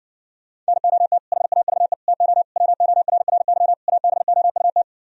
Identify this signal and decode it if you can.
Morse code, heard as a keyed tone, about 69 wpm, 700 Hz. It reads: NOT HAVE TO WORRY ABOUT